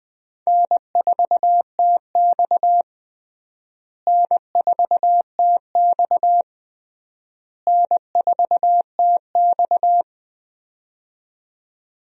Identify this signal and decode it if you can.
Morse code, heard as a keyed tone, about 20 wpm, 700 Hz. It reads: N4TX N4TX N4TX
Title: USA Call Signs - 4 Characters - Repeated 3x - 20wpm